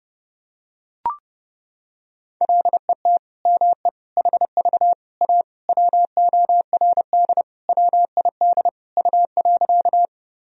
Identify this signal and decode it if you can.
Morse code, heard as a keyed tone, about 31 wpm, 700 Hz. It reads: LET ME HV A WORD WID U.
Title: Sentences from Top 300 Words and Abbreviations 30wpm